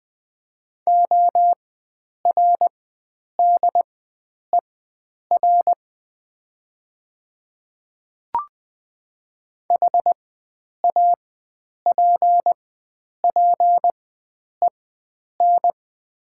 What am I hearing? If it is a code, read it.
Morse code, heard as a keyed tone, about 20 wpm, spaced as for 9 wpm, 700 Hz. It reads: ORDER HAPPEN
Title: 4x Character Spacing - Top 500 Words - 20wpm